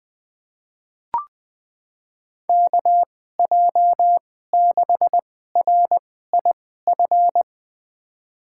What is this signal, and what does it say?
Morse code, heard as a keyed tone, about 20 wpm, spaced as for 14 wpm, 700 Hz. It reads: KJ6RIF
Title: USA Call Signs - 6 Characters - 2x Character Spacing - 20wpm